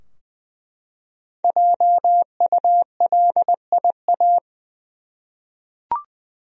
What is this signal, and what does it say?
Morse code, heard as a keyed tone, about 20 wpm, 700 Hz. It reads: JULIA
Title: USA Common Names - Max 5 Letters - 20wpm